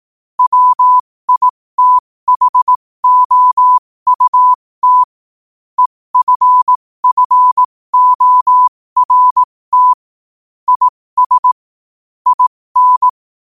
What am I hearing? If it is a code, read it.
Morse code, heard as a keyed tone, about 18 wpm, spaced as for 15 wpm, 1000 Hz. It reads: WITHOUT EFFORT IS IN
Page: New quotes every day in morse code at 15 Words per minute.